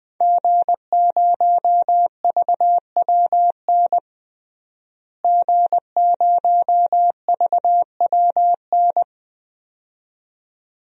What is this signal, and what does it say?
Morse code, heard as a keyed tone, about 20 wpm, 700 Hz. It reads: G0VWN G0VWN
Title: DXCC-100 Call Signs - 5 Characters - Repeated 2x - 20wpm